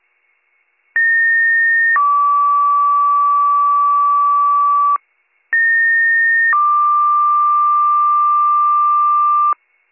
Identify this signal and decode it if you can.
Two-tone page, heard as a one-second tone, then a three-second tone, about 1850 then 1150 Hz, about 1850 then 1180 Hz. It is heard from 1.0 to 9.5 s.